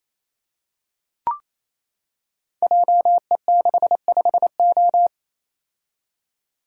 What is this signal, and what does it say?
Morse code, heard as a keyed tone, about 29 wpm, 700 Hz. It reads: JE65O
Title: Code Groups - 5 Letter:Numbers - 28wpm